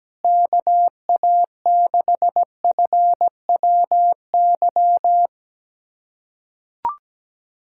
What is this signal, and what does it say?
Morse code, heard as a keyed tone, about 17 wpm, 700 Hz. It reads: KA6FWY
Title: USA Callsign DE Callsign - 6 Characters - 17wpm